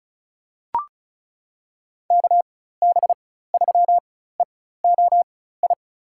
Morse code, heard as a keyed tone, about 36 wpm, spaced as for 16 wpm, 700 Hz. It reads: KB3EOI
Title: USA Call Signs - 6 Characters - 4x Character Spacing - 35wpm